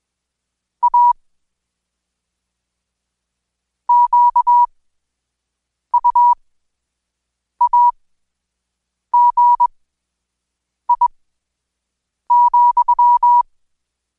Morse code, heard as a keyed tone, about 21 wpm, spaced as for 6 wpm, 970 Hz. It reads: A QUAGI,
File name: Code Quick/AudioDownld2/Seg109-Day14-DayDrill 14